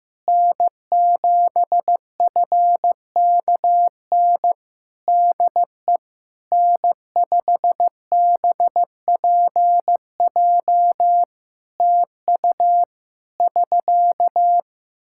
Morse code, heard as a keyed tone, about 15 wpm, 700 Hz. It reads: N7FKN DE N5BPJ TU <SK>
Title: Practice QSO - Exchange 4 - Ending - 15wpm